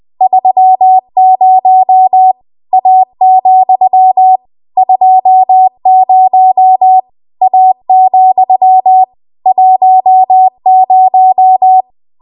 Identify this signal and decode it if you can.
Morse code, heard as a keyed tone, about 20 wpm, 750 Hz. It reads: 30 A, 20 A, 10
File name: Morse / 20WPM